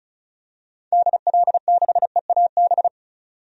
Morse code, heard as a keyed tone, about 34 wpm, 700 Hz. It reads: DL6EAB
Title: DXCC-100 Call Signs - 6 Characters - 35wpm